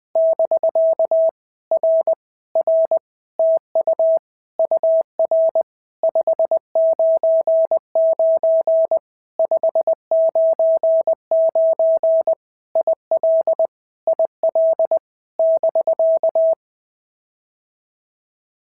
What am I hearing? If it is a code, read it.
Morse code, heard as a keyed tone, about 20 wpm, 650 Hz. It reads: <BK> R R TU UR 599 599 IL IL <BK>
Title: POTA QSO Practice - 20wpm